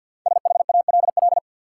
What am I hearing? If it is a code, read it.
Morse code, heard as a keyed tone, about 53 wpm, 700 Hz. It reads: SHALL
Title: Five Letter Words - 50wpm